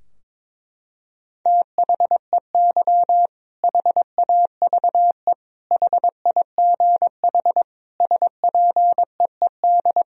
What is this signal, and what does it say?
Morse code, heard as a keyed tone, about 22 wpm, 700 Hz. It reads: THEY HAVE HIGH SPEED
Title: Sentences from Top 1000 Words - Review Part 4 - 22wpm